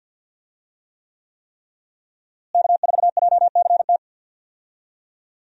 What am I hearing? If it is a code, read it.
Morse code, heard as a keyed tone, about 47 wpm, 700 Hz. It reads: K4JCT
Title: USA Call Signs - 5 Characters - Repeated 3x - 50wpm